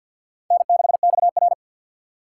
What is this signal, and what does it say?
Morse code, heard as a keyed tone, about 52 wpm, 700 Hz. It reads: N6XR
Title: USA Call Signs - 4 Characters - 6x Character Spacing - 50wpm